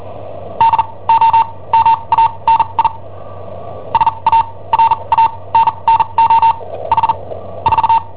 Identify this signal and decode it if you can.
Morse code, heard as a keyed tone, about 41 wpm, spaced as for 26 wpm, 910 Hz. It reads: DOMANI SARANNO H 4
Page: Cassetta esercizi preparata da I7OHP per IW5ELP